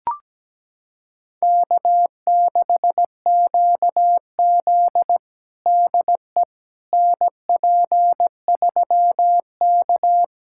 Morse code, heard as a keyed tone, about 17 wpm, 700 Hz. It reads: K6QZ DE NP3K
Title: USA Callsign DE Callsign - 4 Characters - 17wpm